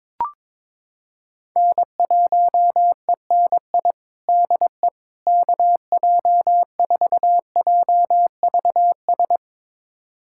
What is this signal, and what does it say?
Morse code, heard as a keyed tone, about 22 wpm, 700 Hz. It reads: N1ENI DE KJ4JVS
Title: USA Callsign DE Callsign 22wpm